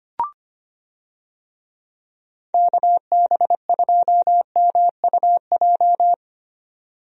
Morse code, heard as a keyed tone, about 25 wpm, 700 Hz. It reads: KB2MUJ
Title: USA Call Signs - 6 Characters - Repeated 3x - 25wpm